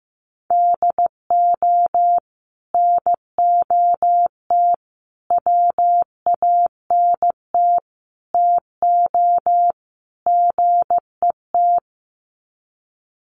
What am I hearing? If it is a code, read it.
Morse code, heard as a keyed tone, about 15 wpm, 700 Hz. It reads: DO NOT WANT TO GET
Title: Sets of 5 Words 15wpm